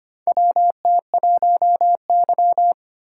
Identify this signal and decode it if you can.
Morse code, heard as a keyed tone, about 25 wpm, 700 Hz. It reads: WT1Y
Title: USA Call Signs - 6x Character Spacing - 25wpm